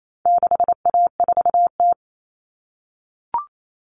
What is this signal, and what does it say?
Morse code, heard as a keyed tone, about 28 wpm, 700 Hz. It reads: 6A4T
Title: All Cut Numbers - 4-Digit 28wpm